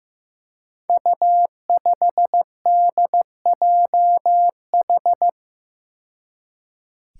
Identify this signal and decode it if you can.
Morse code, heard as a keyed tone, about 15 wpm, 700 Hz. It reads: U5DJH
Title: Five Challenging Letter-Number Combinations 15wpm